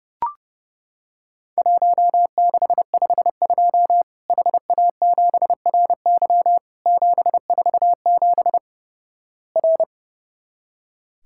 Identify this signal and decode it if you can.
Morse code, heard as a keyed tone, about 31 wpm, 700 Hz. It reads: 1652 HA7RY 747 R
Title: Worked All Europe DX Contest - QTC Exchange - 30wpm